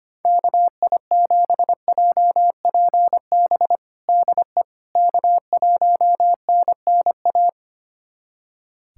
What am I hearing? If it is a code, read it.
Morse code, heard as a keyed tone, about 25 wpm, 700 Hz. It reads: KI7JPB DE K1NNA
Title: USA Callsign DE Callsign 25wpm